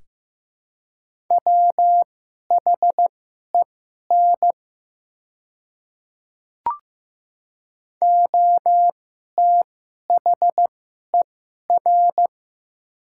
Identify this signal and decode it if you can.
Morse code, heard as a keyed tone, about 15 wpm, spaced as for 11 wpm, 700 Hz. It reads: WHEN OTHER
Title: 2x Character Spacing - Top 100 Words - 15wpm